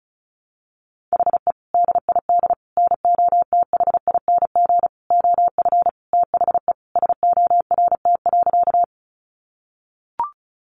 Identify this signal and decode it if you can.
Morse code, heard as a keyed tone, about 34 wpm, 700 Hz. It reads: HE DID NOTHING OF THE SORT.
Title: Sentences from Top 1300 Words 35wpm